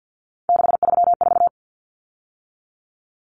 Morse code, heard as a keyed tone, about 48 wpm, 700 Hz. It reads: 634
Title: Common Cut Numbers - 3-Digit 50wpm